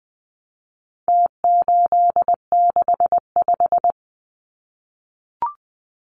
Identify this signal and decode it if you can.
Morse code, heard as a keyed tone, about 20 wpm, 700 Hz. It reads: T865
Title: Numbers - Contest Style - 4-Digit 20wpm